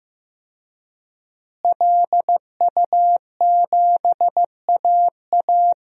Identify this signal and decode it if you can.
Morse code, heard as a keyed tone, about 15 wpm, 700 Hz. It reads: LU7AA
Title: DXCC-100 Call Signs - 5 Characters - Repeated 3x - 15wpm